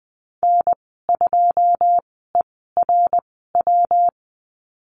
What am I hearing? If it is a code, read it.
Morse code, heard as a keyed tone, about 20 wpm, spaced as for 14 wpm, 700 Hz. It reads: N2ERW
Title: 193 Lesson 31g - Letter v - Call Signs 20wpm